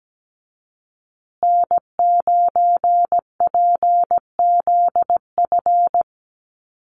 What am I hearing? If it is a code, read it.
Morse code, heard as a keyed tone, about 17 wpm, 700 Hz. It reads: N9PZF
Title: USA Call Signs Repeated 3x 17wpm